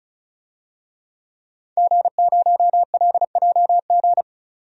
Morse code, heard as a keyed tone, about 36 wpm, 700 Hz. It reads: G0LJG
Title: DXCC-100 Call Signs - 5 Characters - 35wpm